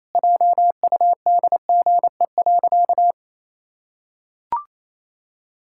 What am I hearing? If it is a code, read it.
Morse code, heard as a keyed tone, about 28 wpm, 700 Hz. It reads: JUDGE.
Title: Pangram Challenge v2 - 28wpm